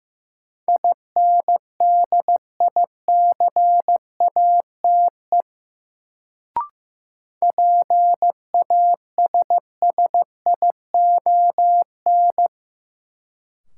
Morse code, heard as a keyed tone, about 16 wpm, 700 Hz. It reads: INDICATE PASSION
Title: Sentences from Top 1300 Words - New Word Drill Alt - 15wpm